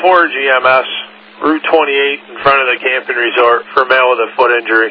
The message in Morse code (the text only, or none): none